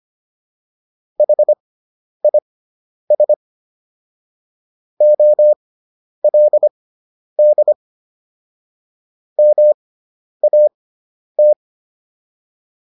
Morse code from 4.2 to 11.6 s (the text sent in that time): OLD MAT